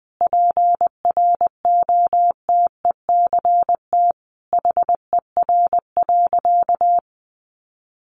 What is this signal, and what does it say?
Morse code, heard as a keyed tone, about 20 wpm, 700 Hz. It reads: PROTECT HER.
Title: Sentences from Top 900 Words 20wpm